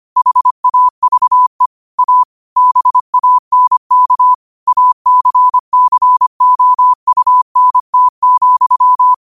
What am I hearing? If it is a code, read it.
Morse code, heard as a keyed tone, about 25 wpm, 1000 Hz. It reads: HAVE A BANK ACCOUNT,
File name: qotdfiles / 25wpm